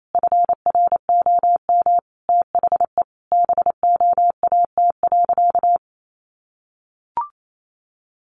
Morse code, heard as a keyed tone, about 28 wpm, 700 Hz. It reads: FROM THE BOAT.
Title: Sentences from Top 1400 Words 28wpm